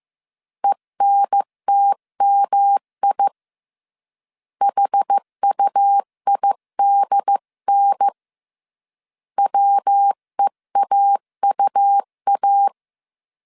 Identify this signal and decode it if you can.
Morse code, heard as a keyed tone, about 15 wpm, 800 Hz. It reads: ENTMI HUIDN WEAUA
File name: CW / DET / 1 3 1500C S C 14wpm 2458